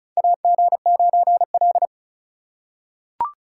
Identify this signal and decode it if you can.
Morse code, heard as a keyed tone, about 34 wpm, 700 Hz. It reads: AG9L
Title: USA Call Signs - 4 Characters - 3x Character Spacing - 35wpm